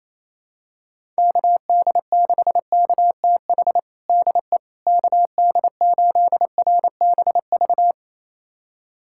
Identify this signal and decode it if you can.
Morse code, heard as a keyed tone, about 28 wpm, 700 Hz. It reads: KD6KTH DE KD8RBV
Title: USA Callsign DE Callsign - 6 Characters - 28wpm